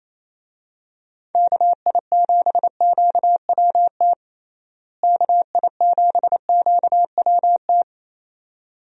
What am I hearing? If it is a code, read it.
Morse code, heard as a keyed tone, about 28 wpm, 700 Hz. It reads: KI7QWT KI7QWT
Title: USA Call Signs - 6 Characters - Repeated 2x - 28wpm